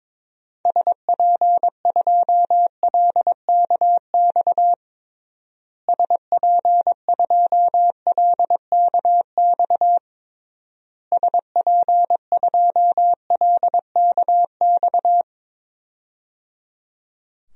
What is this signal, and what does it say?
Morse code, heard as a keyed tone, about 22 wpm, 700 Hz. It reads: SP2LKX SP2LKX SP2LKX
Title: DXCC-100 Call Signs - 6 Characters - Repeated 3x - 22wpm